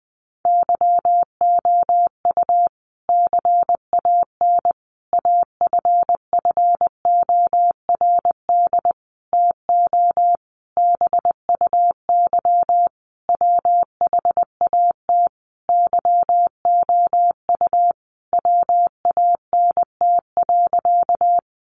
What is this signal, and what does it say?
Morse code, heard as a keyed tone, about 20 wpm, 700 Hz. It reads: YOU CAN AFFORD TO BUY WHAT YOU WANT.
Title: Sentences from Top 1100 Words 20wpm